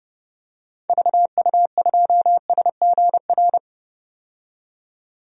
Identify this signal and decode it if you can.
Morse code, heard as a keyed tone, about 30 wpm, 700 Hz. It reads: VU2SGR
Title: DXCC-100 Call Signs - 6 Characters - 30wpm